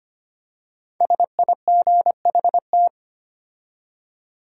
Morse code, heard as a keyed tone, about 25 wpm, 700 Hz. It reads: SIGHT
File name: 5 Letter Words - First 3 Letters Given - 25wpm